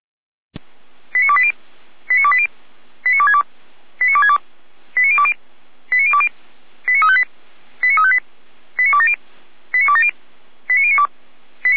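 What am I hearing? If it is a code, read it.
Five-tone selective call (ZVEI), heard as quick runs of five tones, tones of about 80 ms. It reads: 89280 89280 89262 89262 89020 89020 89358 89358 89270 89270 89092
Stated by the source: ZVEI five-tone selective call received by SDR.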